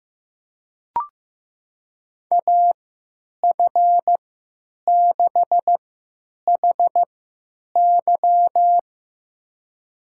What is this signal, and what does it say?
Morse code, heard as a keyed tone, about 15 wpm, spaced as for 9 wpm, 700 Hz. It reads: AF6HY
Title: USA Call Signs - 5 Characters - 3x Character Spacing - 15wpm